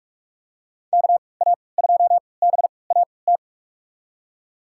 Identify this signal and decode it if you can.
Morse code, heard as a keyed tone, about 44 wpm, spaced as for 25 wpm, 700 Hz. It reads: KA2BAT
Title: USA Call Signs - 6 Characters - 3x Character Spacing - 45wpm